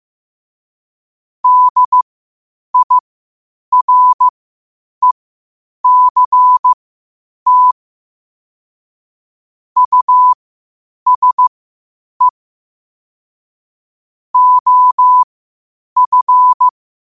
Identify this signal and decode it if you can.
Morse code, heard as a keyed tone, about 15 wpm, spaced as for 8 wpm, 1000 Hz. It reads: DIRECT USE OF
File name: 8wpm